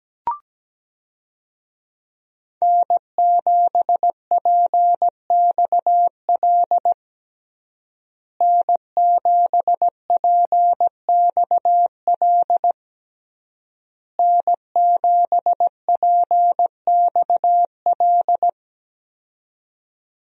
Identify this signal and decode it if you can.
Morse code, heard as a keyed tone, about 17 wpm, 700 Hz. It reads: N7PXL N7PXL N7PXL
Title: USA Call Signs - 5 Characters - Repeated 3x - 17wpm